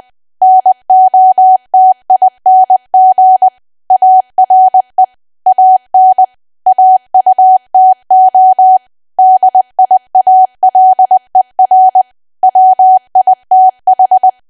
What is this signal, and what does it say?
Morse code, heard as a keyed tone, about 20 wpm, 750 Hz. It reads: NOTING ARE AN AUTO DIALER WITH
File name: Archive / 20 WPM / 20WPM